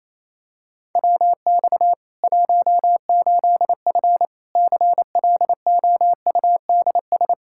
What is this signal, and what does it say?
Morse code, heard as a keyed tone, about 28 wpm, 700 Hz. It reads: WX 18F CLOUDS
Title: Practice QSO - Exchange 2 - Info - 28wpm